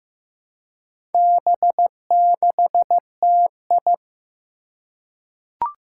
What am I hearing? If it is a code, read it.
Morse code, heard as a keyed tone, about 15 wpm, 700 Hz. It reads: B6TI
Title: Four Challenging Letter-Number Combinations 15wpm